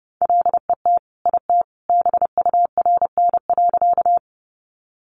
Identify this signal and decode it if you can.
Morse code, heard as a keyed tone, about 30 wpm, 700 Hz. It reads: LET IT BURN.
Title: Sentences from Top 1300 Words 30wpm